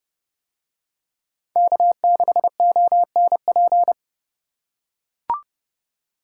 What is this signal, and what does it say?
Morse code, heard as a keyed tone, about 29 wpm, 700 Hz. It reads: K6ONP